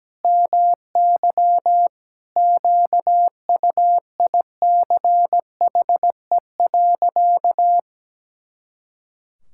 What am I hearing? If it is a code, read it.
Morse code, heard as a keyed tone, about 17 wpm, 700 Hz. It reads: MY QUICHE.
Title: Pangram Challenge v2 - 17wpm